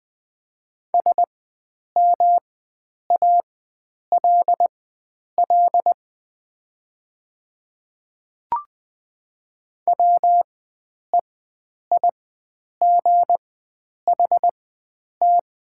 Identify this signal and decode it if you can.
Morse code, heard as a keyed tone, about 21 wpm, spaced as for 9 wpm, 700 Hz. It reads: SMALL WEIGHT
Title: 4x Character Spacing - Top 500 Words - 20wpm